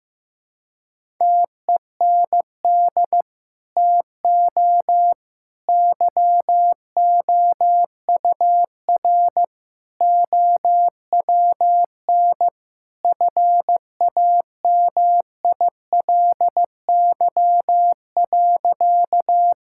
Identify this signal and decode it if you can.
Morse code, heard as a keyed tone, about 15 wpm, 700 Hz. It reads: TEND TO YOUR OWN FAMILY.
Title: Sentences from Top 1000 Words - Review Part 5 - 15wpm